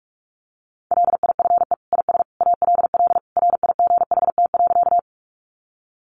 Morse code, heard as a keyed tone, about 44 wpm, 700 Hz. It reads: LIFE IS ALL RIGHT.
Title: Sentences from Top 300 Words 45wpm